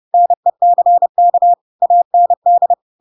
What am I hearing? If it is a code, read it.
Morse code, heard as a keyed tone, about 30 wpm, 700 Hz. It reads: NECK AND
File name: Pinocchio / Pinocchio ch8 30 30 ik1hge